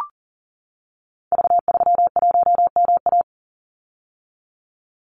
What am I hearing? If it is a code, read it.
Morse code, heard as a keyed tone, about 40 wpm, 700 Hz. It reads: V31MA
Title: Call Signs - 40wpm